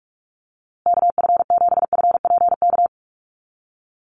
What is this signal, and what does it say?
Morse code, heard as a keyed tone, about 44 wpm, 700 Hz. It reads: KF7FPK
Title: USA Call Signs Repeated 2x 45wpm